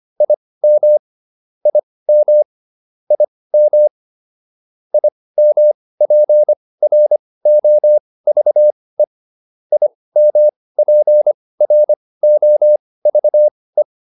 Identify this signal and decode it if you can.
Morse code, heard as a keyed tone, about 25 wpm, spaced as for 18 wpm, 600 Hz. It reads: IM IM IM IMPROVE IMPROVE